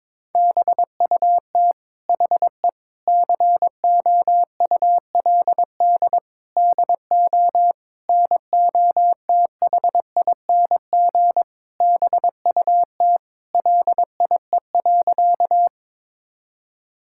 Morse code, heard as a keyed tone, about 22 wpm, 700 Hz. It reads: BUT HE COULD DO NOTHING BUT LIE.
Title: Sentences from Top 1600 Words 22wpm